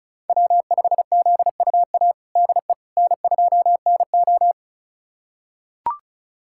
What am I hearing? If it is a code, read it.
Morse code, heard as a keyed tone, about 36 wpm, 700 Hz. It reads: W5ZUA DE N2NO
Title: USA Callsign DE Callsign 35wpm